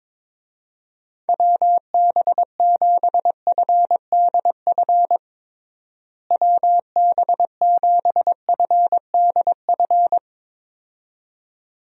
Morse code, heard as a keyed tone, about 22 wpm, 700 Hz. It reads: WB7FDF WB7FDF
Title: USA Call Signs - 6 Characters - Repeated 2x - 22wpm